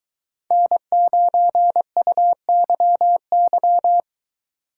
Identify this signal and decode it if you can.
Morse code, heard as a keyed tone, about 23 wpm, 700 Hz. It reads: N9UYY